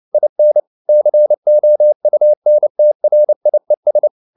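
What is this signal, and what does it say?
Morse code, heard as a keyed tone, about 29 wpm, 600 Hz. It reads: IN COUNTRIES
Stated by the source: Morse code at 29 wpm, sending Ken Follett's novel Never in English, from CW med Gnister.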